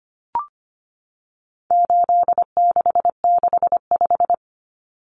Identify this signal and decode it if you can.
Morse code, heard as a keyed tone, about 25 wpm, 700 Hz. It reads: 8665
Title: Numbers - 4-Digit 25wpm